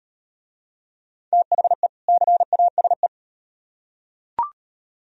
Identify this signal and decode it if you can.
Morse code, heard as a keyed tone, about 38 wpm, 700 Hz. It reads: THE CASE